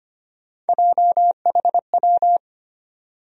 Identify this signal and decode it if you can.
Morse code, heard as a keyed tone, about 25 wpm, 700 Hz. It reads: JHW
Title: USA Call Sign Postfixes - 3 Letters 25wpm